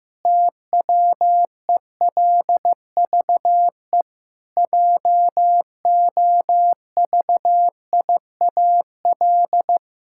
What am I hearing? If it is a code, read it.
Morse code, heard as a keyed tone, about 15 wpm, 700 Hz. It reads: TWELVE JOVIAL
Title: Pangram Challenge v2 - 15wpm